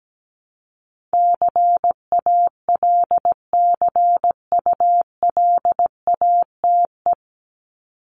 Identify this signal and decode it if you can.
Morse code, heard as a keyed tone, about 17 wpm, 700 Hz. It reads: CALCULATE
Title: Top 1751-2000 Words 17wpm